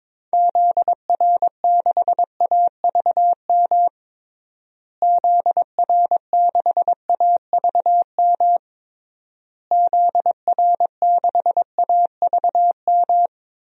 Morse code, heard as a keyed tone, about 21 wpm, 700 Hz. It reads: ZR6AVM ZR6AVM ZR6AVM
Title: DXCC-100 Call Signs - 6 Characters - Repeated 3x - 22wpm